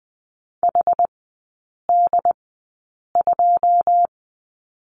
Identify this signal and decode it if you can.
Morse code, heard as a keyed tone, about 20 wpm, spaced as for 8 wpm, 700 Hz. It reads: HD2
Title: 114 Lesson 19e - Letter w - Three Characters 20wpm